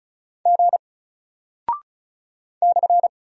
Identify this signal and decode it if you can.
Morse code, heard as a keyed tone, about 36 wpm, 700 Hz. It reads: G /
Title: Single Character (Letter:Number:Punctuation:Prosign) Alt 35wpm